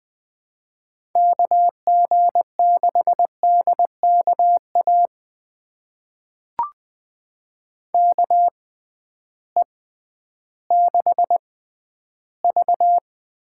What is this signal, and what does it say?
Morse code, heard as a keyed tone, about 20 wpm, spaced as for 12 wpm, 700 Hz. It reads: KG6DKA KE6V
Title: USA Call Signs - 6 Characters - 6x Character Spacing - 20wpm